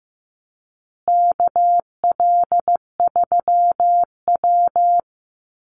Morse code, heard as a keyed tone, about 15 wpm, 700 Hz. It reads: KL3W
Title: Call Signs Repeated 2x - 15wpm